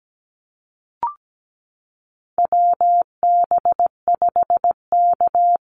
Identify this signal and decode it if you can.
Morse code, heard as a keyed tone, about 17 wpm, 700 Hz. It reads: WB5K
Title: Call Signs Repeated 2x - 17wpm